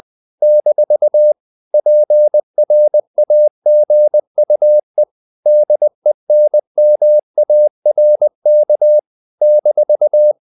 Morse code, heard as a keyed tone, about 20 wpm, 600 Hz. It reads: - PRAGUE DENMARK -
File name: CW / Practice / NationalCapitals20